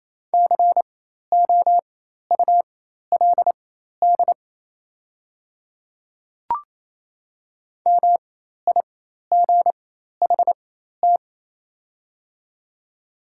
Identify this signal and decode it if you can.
Morse code, heard as a keyed tone, about 28 wpm, spaced as for 13 wpm, 700 Hz. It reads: COULD MIGHT
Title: 4x Character Spacing - Top 500 Words - 28wpm